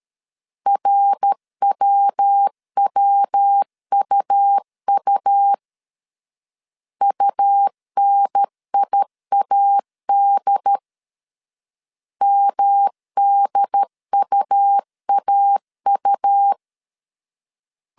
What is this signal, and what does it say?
Morse code, heard as a keyed tone, about 13 wpm, 800 Hz. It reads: RWWUU UNIAD MDUAU
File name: CW / DET / 1 3 1500C S C 12wpm 2750